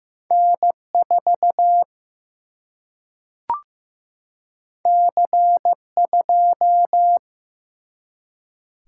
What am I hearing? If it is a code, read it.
Morse code, heard as a keyed tone, about 15 wpm, 700 Hz. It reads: N 4 C 2
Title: Single-Letter Single-Number 15wpm